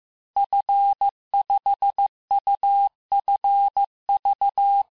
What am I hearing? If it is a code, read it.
Morse code, heard as a keyed tone, about 15 wpm, 800 Hz. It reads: F5UFV
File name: cours-du-lundi files / cours-1585-texte-FIN-de-QSO-vitesse-14mpm